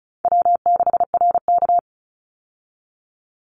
Morse code, heard as a keyed tone, about 34 wpm, 700 Hz. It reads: W6RK
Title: Easy Call Signs Repeated 2x - 35wpm